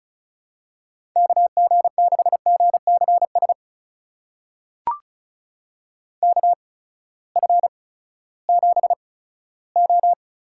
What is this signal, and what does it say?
Morse code, heard as a keyed tone, about 34 wpm, spaced as for 17 wpm, 700 Hz. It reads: KG6GCS KF7O
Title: USA Call Signs - 6 Characters - 8x Character Spacing - 35wpm